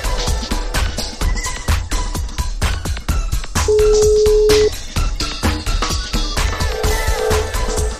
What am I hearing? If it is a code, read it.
Ringback tone (Europe, 425 Hz).